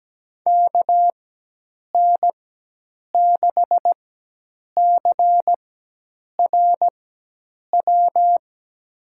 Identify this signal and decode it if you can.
Morse code, heard as a keyed tone, about 17 wpm, spaced as for 8 wpm, 700 Hz. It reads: KN6CRW
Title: USA Call Signs - 6 Characters - 4x Character Spacing - 17wpm